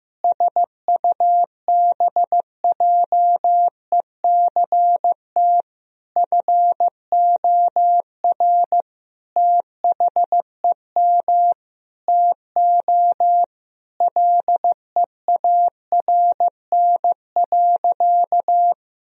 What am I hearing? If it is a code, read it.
Morse code, heard as a keyed tone, about 16 wpm, 700 Hz. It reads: SUBJECT FOR THEM TO LEARN.
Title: Sentences from Top 1000 Words - Review Part 3 - 15wpm